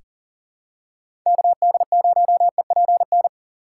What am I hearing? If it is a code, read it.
Morse code, heard as a keyed tone, about 40 wpm, 700 Hz. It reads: KD0EPN
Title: USA Call Signs - 6 Characters - 2x Character Spacing - 40wpm